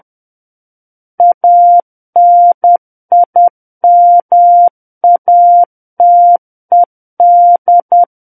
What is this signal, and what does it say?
Morse code, heard as a keyed tone, about 10 wpm, 700 Hz. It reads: ANIMATED
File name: Book One / 01-Chapter 1 - 10 WPM